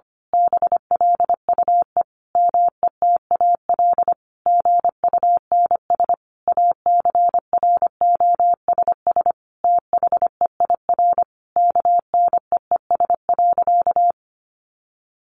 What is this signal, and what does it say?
Morse code, heard as a keyed tone, about 25 wpm, 700 Hz. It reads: BLUE METAL GUNS ACROSS THEIR KNEES.
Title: A Sound Of Thunder 25wpm